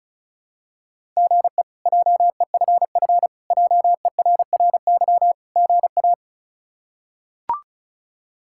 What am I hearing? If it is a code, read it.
Morse code, heard as a keyed tone, about 36 wpm, 700 Hz. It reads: GE JEFF JERRY GA
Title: K1USN SST QSO Practice - 35wpm